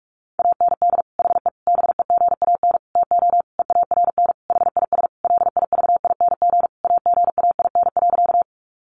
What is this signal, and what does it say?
Morse code, heard as a keyed tone, about 44 wpm, 700 Hz. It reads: AND HE BEGAN TO EARN HIS LIVING AGAIN.
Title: Sentences from Top 1500 Words 45wpm